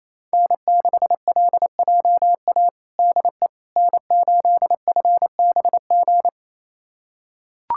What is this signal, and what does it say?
Morse code, heard as a keyed tone, about 28 wpm, 700 Hz. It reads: N6LJA DE N8FBG
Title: USA Callsign DE Callsign - 5 Characters - 28wpm